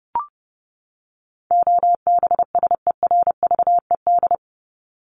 Morse code, heard as a keyed tone, about 30 wpm, 700 Hz. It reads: OBSERVED
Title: ED Suffix - Top 100 Words with Base Words 30wpm